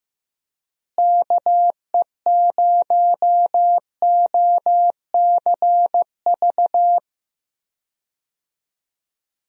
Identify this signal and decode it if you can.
Morse code, heard as a keyed tone, about 15 wpm, 700 Hz. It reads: KE0OCV
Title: USA Call Signs - 6 Characters - Repeated 2x - 15wpm